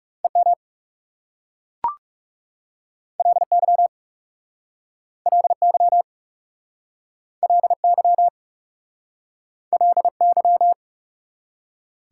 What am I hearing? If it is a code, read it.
Morse code, heard as a keyed tone, about 40 wpm, 690 Hz. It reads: EM LY LY LY LY